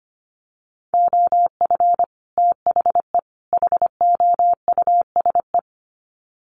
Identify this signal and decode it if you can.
Morse code, heard as a keyed tone, about 25 wpm, 700 Hz. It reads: OF THE HOUSE
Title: Sets of 3 Words 25wpm